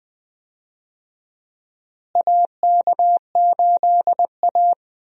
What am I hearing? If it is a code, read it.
Morse code, heard as a keyed tone, about 20 wpm, 700 Hz. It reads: AK8A
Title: USA Call Signs - 4 Characters - Repeated 3x - 20wpm